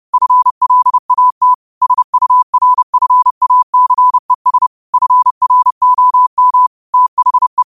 Morse code, heard as a keyed tone, about 30 wpm, 1000 Hz. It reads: FLAT SURFACES FROM THE
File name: qotdfiles / 30wpm